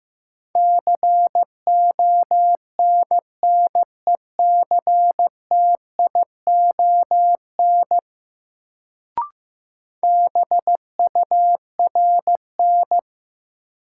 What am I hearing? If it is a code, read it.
Morse code, heard as a keyed tone, about 15 wpm, 700 Hz. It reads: CONNECTION BURN
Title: Sentences from Top 1300 Words - New Word Drill Alt - 15wpm